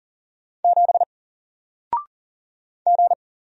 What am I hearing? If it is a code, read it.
Morse code, heard as a keyed tone, about 39 wpm, 700 Hz. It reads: 7 G
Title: Single Character (Letter:Number:Punctuation:Prosign) 40wpm